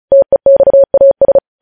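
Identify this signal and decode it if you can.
Morse code, heard as a keyed tone, about 36 wpm, 550 Hz. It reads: TEXAS
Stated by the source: Morse code practice of news headlines at 35 wpm, opening with vvv.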